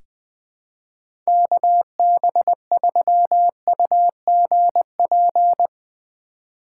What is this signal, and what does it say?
Morse code, heard as a keyed tone, about 20 wpm, 700 Hz. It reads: KB3UGP
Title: USA Call Signs - 6 Characters - 3x Character Spacing - 20wpm